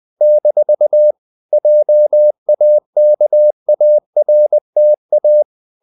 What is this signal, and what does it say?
Morse code, heard as a keyed tone, about 20 wpm, 600 Hz. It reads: - JAKARTA
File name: CW / Practice / NationalCapitals20